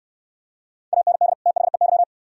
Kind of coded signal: Morse code

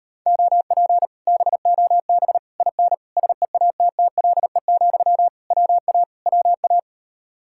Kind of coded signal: Morse code

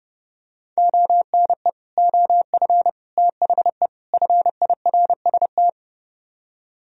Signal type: Morse code

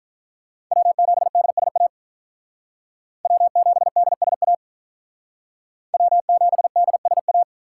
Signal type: Morse code

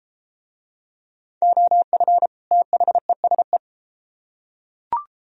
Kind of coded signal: Morse code

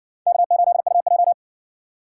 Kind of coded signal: Morse code

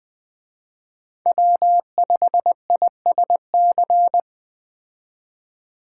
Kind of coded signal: Morse code

